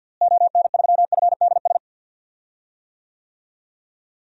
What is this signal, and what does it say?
Morse code, heard as a keyed tone, about 47 wpm, 700 Hz. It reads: ON3FDS
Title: DXCC-100 Call Signs - 6 Characters - Repeated 2x - 50wpm